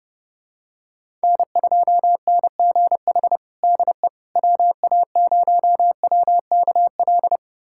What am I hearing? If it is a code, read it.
Morse code, heard as a keyed tone, about 32 wpm, 700 Hz. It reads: N2NGH DE WA0WKL
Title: USA Callsign DE Callsign 30wpm